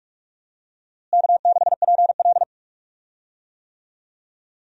Morse code, heard as a keyed tone, about 46 wpm, 700 Hz. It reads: K6PL
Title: USA Call Signs - 4 Characters - Repeated 2x - 45wpm